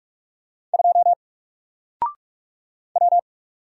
Morse code, heard as a keyed tone, about 44 wpm, 700 Hz. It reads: 2 W
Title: Single Letter-Number Alt 45wpm